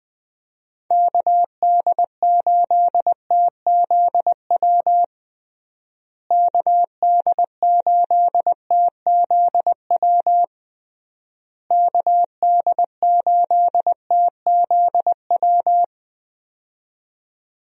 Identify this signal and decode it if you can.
Morse code, heard as a keyed tone, about 20 wpm, 700 Hz. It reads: KD8TZW KD8TZW KD8TZW
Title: USA Call Signs - 6 Characters - Repeated 3x - 20wpm